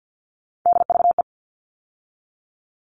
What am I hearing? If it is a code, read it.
Morse code, heard as a keyed tone, about 50 wpm, 700 Hz. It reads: DVE
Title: All Cut Numbers - 3-Digit 50wpm